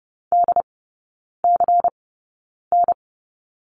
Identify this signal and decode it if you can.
Morse code, heard as a keyed tone, about 30 wpm, spaced as for 9 wpm, 690 Hz. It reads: DCN
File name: Lesson 16e - Letter c - Three Characters 30wpm